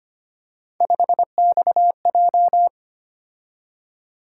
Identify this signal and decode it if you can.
Morse code, heard as a keyed tone, about 25 wpm, 700 Hz. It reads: 5XJ
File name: Three Letter-Number Combinations - 25wpm